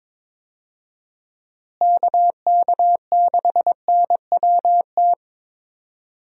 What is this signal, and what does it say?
Morse code, heard as a keyed tone, about 22 wpm, 700 Hz. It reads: KK6NWT
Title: USA Call Signs - 6 Characters - Repeated 2x - 22wpm